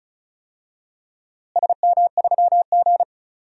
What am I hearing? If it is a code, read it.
Morse code, heard as a keyed tone, about 36 wpm, 700 Hz. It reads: SM3G
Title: DXCC-100 Call Signs - 4 Characters - Repeated 2x - 35wpm